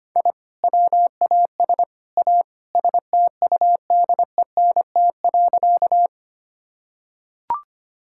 Morse code, heard as a keyed tone, about 25 wpm, 700 Hz. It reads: I WAS A STUDENT.
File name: practice / 25wpm / Sentences with 3-Word Phrases - 25wpm